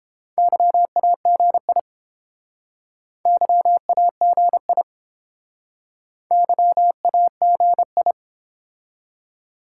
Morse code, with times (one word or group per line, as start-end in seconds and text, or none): 0.4-1.8 YAGI
3.3-4.8 YAGI
6.3-8.1 YAGI